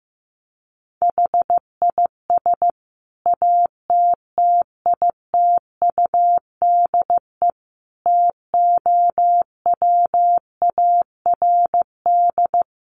Morse code, heard as a keyed tone, about 15 wpm, 700 Hz. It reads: HIS ATTITUDE TOWARD
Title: Sentences from Top 1300 Words 15wpm